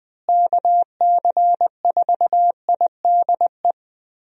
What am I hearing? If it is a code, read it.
Morse code, heard as a keyed tone, about 20 wpm, 700 Hz. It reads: KC4IDE
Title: USA Call Signs - 6 Characters - Repeated 2x - 20wpm